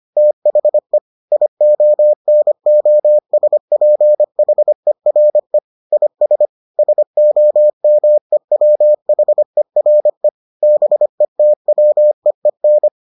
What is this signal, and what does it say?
Morse code, heard as a keyed tone, about 25 wpm, 600 Hz. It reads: THE IONOSPHERE IS SOMEWHERE BETWEEN